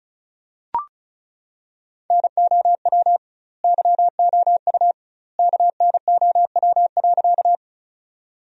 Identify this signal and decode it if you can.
Morse code, heard as a keyed tone, about 36 wpm, 700 Hz. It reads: NOW YOU KNOW.
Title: 2x Word-Spacing - Sentences from Top 100 Words - 35wpm